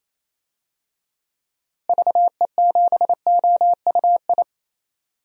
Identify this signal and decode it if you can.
Morse code, heard as a keyed tone, about 28 wpm, 700 Hz. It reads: VE7OUI
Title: DXCC-100 Call Signs - 6 Characters - Repeated 3x - 28wpm